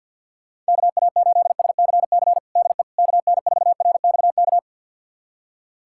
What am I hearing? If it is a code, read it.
Morse code, heard as a keyed tone, about 46 wpm, 700 Hz. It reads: KA8SCC DE KN4RXK
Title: USA Callsign DE Callsign 50wpm